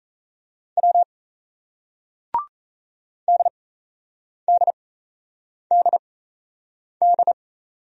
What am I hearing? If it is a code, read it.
Morse code, heard as a keyed tone, about 38 wpm, spaced as for 8 wpm, 700 Hz. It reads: W DDDD